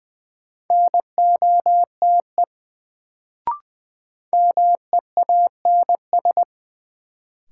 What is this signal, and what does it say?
Morse code, heard as a keyed tone, about 20 wpm, 700 Hz. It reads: NOTE MEANS
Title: Sets of 5 Words - New Word Drill - 20wpm